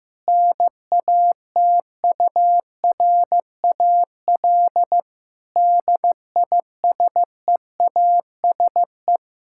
Morse code, heard as a keyed tone, about 15 wpm, 700 Hz. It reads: NATURAL DISEASE
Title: Sentences from Top 1000 Words - Review Part 6 - 15wpm